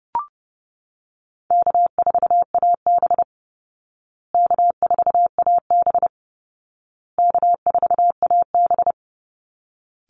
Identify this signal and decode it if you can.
Morse code, heard as a keyed tone, about 30 wpm, 700 Hz. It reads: K4AB K4AB K4AB
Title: Call Signs Repeated 2x - 30wpm